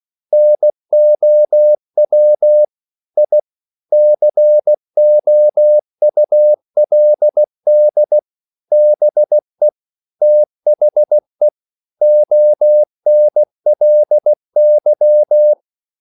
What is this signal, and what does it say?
Morse code, heard as a keyed tone, about 16 wpm, 600 Hz. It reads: NOW I COULD BE THE ONLY